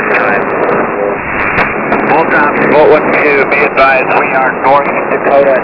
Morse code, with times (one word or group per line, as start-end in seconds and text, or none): none